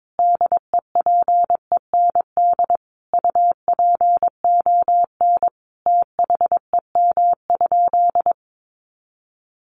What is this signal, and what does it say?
Morse code, heard as a keyed tone, about 22 wpm, 700 Hz. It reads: DEPEND UPON THEM?
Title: Sentences from Top 1300 Words 22wpm